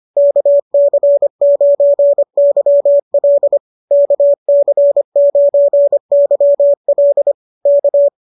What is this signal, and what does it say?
Morse code, heard as a keyed tone, about 25 wpm, 590 Hz. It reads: KC9YL KC9YL K